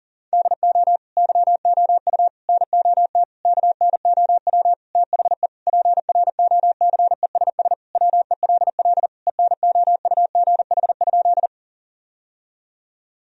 Morse code, heard as a keyed tone, about 37 wpm, 700 Hz. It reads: DO YOU NOT KNOW THE PROCESS WELL ENOUGH?
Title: Sentences from Top 1000 Words - Review Part 3 - 40wpm